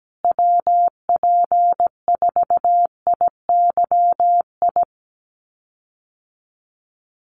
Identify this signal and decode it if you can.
Morse code, heard as a keyed tone, about 17 wpm, 700 Hz. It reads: WP4IYI
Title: USA Call Signs Repeated 3x 17wpm